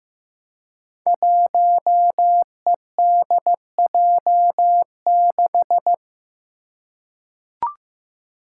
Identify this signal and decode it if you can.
Morse code, heard as a keyed tone, about 15 wpm, 700 Hz. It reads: 1EDJ6
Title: Five Challenging Letter-Number Combinations 15wpm